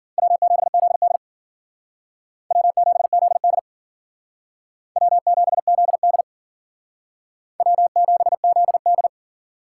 Morse code, heard as a keyed tone, about 52 wpm, 700 Hz. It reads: W7ZD W7ZD W7ZD W7ZD